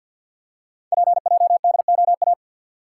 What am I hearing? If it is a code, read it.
Morse code, heard as a keyed tone, about 47 wpm, 700 Hz. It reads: PJDOA
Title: Code Groups - 5 Letters - 50wpm